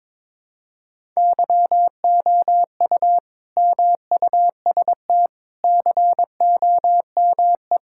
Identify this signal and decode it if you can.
Morse code, heard as a keyed tone, about 22 wpm, 700 Hz. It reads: YOU MUST COME
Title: Sentences from Top 1000 Words - Review Part 5 - 22wpm